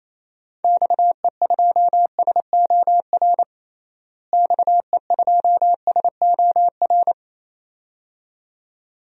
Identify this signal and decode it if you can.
Morse code, heard as a keyed tone, about 28 wpm, 700 Hz. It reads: XE2SOR XE2SOR
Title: DXCC-100 Call Signs - 6 Characters - Repeated 2x - 28wpm